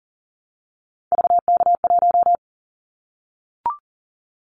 Morse code, heard as a keyed tone, about 40 wpm, 700 Hz. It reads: VK1
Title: Callsign Prefixes for Top 100 DXCC - 40wpm